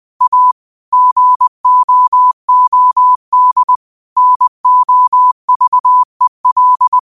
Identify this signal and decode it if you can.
Morse code, heard as a keyed tone, about 20 wpm, 1000 Hz. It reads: A GOOD NOVEL